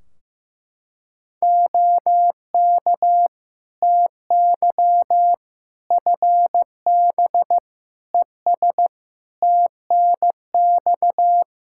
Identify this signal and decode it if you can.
Morse code, heard as a keyed tone, about 15 wpm, 700 Hz. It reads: OK TY FB ES TNX
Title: K1USN SST QSO Practice 15wpm